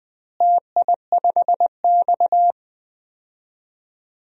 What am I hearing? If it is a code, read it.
Morse code, heard as a keyed tone, about 21 wpm, 700 Hz. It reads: TI5X
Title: DXCC-100 Call Signs - 4 Characters - Repeated 3x - 20wpm